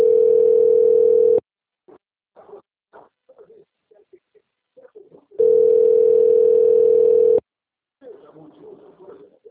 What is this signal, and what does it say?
Ringback tone (North America), 440 + 480 Hz, with speech.
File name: out- - - - - .21819